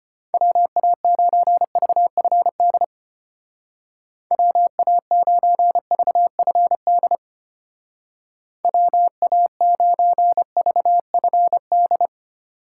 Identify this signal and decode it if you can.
Morse code, heard as a keyed tone, about 33 wpm, 700 Hz. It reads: WA9VFD WA9VFD WA9VFD